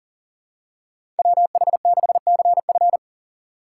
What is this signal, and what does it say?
Morse code, heard as a keyed tone, about 40 wpm, 700 Hz. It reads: WH6CF
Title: USA Call Signs - 5 Characters - 3x Character Spacing - 40wpm